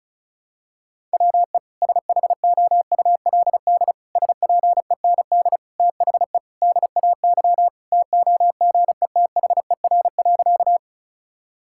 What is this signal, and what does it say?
Morse code, heard as a keyed tone, about 34 wpm, 700 Hz. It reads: WE SHOULD SPEND THE DAY TOGETHER.
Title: Sentences from Top 1000 Words - Review Part 5 - 35wpm